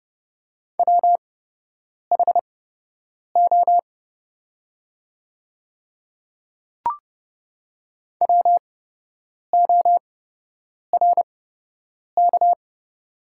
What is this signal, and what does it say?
Morse code, heard as a keyed tone, about 30 wpm, spaced as for 8 wpm, 700 Hz. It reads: WHO WORK
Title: 8x Character Spacing - Top 300 Words - 30wpm